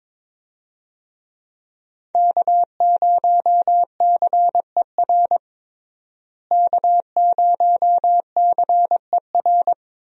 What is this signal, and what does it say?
Morse code, heard as a keyed tone, about 21 wpm, 700 Hz. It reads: K0CER K0CER
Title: USA Call Signs - 5 Characters - Repeated 2x - 22wpm